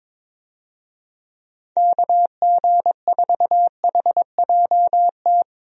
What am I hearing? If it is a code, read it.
Morse code, heard as a keyed tone, about 22 wpm, 700 Hz. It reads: KG4HJT
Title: USA Call Signs - 6 Characters - Repeated 3x - 22wpm